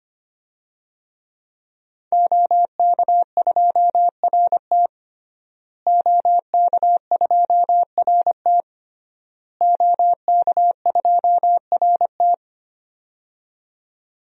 Morse code, heard as a keyed tone, about 25 wpm, 700 Hz. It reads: OK2RT OK2RT OK2RT
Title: DXCC-100 Call Signs - 5 Characters - Repeated 3x - 25wpm